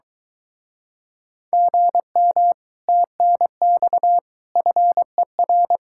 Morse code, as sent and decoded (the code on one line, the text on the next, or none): --. -- / - -. -..- / ..-. .-.
GM TNX FER